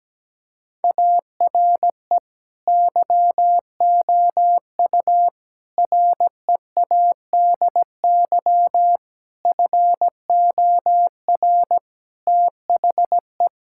Morse code, heard as a keyed tone, about 17 wpm, 700 Hz. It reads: ARE YOU READY FOR THE
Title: Sentences from Top 1000 Words - Review Part 6 - 17wpm